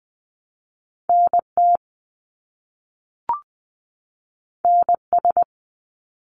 Morse code, heard as a keyed tone, about 20 wpm, 700 Hz. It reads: NT NS
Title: Canadian Provinces and Territories 20wpm